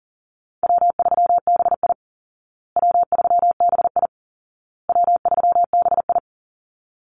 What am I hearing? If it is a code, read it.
Morse code, heard as a keyed tone, about 37 wpm, 700 Hz. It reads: W3BI W3BI W3BI
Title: Easy Call Signs Repeated 3x - 40wpm